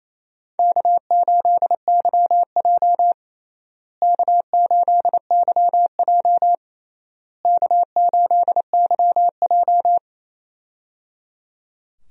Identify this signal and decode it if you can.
Morse code, heard as a keyed tone, about 28 wpm, 700 Hz. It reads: K8YJ K8YJ K8YJ
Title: USA Call Signs - 4 Characters - Repeated 3x - 28wpm